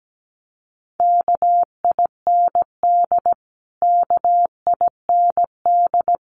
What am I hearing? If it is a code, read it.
Morse code, heard as a keyed tone, about 17 wpm, 700 Hz. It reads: KIND KIND
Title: Top 500 Words Repeated 17wpm